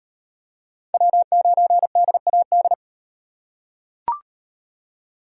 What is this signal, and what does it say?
Morse code, heard as a keyed tone, about 38 wpm, 700 Hz. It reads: W9DAD